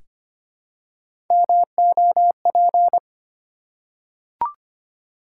Morse code, heard as a keyed tone, about 25 wpm, 700 Hz. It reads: MOP
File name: practice / 25wpm / Easy Words with Hints - 25wpm